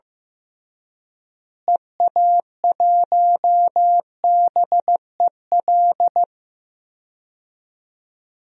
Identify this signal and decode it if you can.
Morse code, heard as a keyed tone, about 15 wpm, 700 Hz. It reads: EA1BEL
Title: DXCC-100 Call Signs - 6 Characters - Repeated 2x - 15wpm